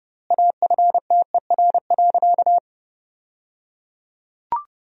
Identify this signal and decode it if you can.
Morse code, heard as a keyed tone, about 30 wpm, 700 Hz. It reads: AFTER.
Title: 3x Word-Spacing - Sentences from Top 500 Words - 30wpm